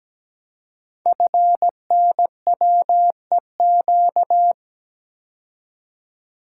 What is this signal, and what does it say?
Morse code, heard as a keyed tone, about 17 wpm, 700 Hz. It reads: FNWEQ
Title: Code Groups - 5 Letters - 17wpm